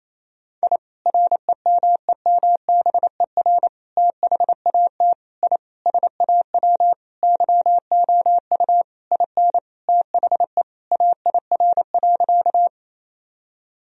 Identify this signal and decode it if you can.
Morse code, heard as a keyed tone, about 28 wpm, 700 Hz. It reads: I REMEMBER THAT I SAW YOU IN THE AIR.
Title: Sentences from Top 1000 Words - Review Part 1 - 28wpm